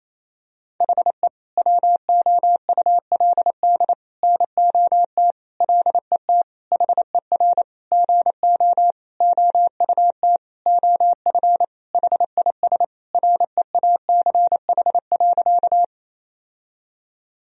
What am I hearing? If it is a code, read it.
Morse code, heard as a keyed tone, about 28 wpm, 700 Hz. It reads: HE WOULD NOT LET HER GO OUT OF HIS REACH.
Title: Sentences from Top 1000 Words - Review Part 2 - 28wpm